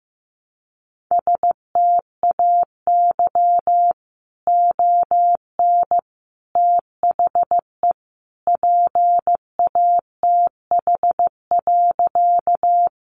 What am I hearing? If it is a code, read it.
Morse code, heard as a keyed tone, about 15 wpm, 700 Hz. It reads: STAY ON THE PATH.
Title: A Sound Of Thunder 15wpm